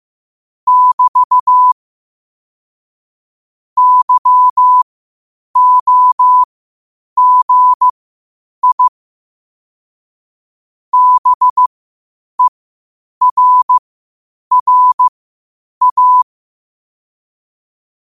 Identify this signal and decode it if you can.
Morse code, heard as a keyed tone, about 15 wpm, spaced as for 8 wpm, 1000 Hz. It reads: <BT> YOGI BERRA
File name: qotdfiles / 8wpm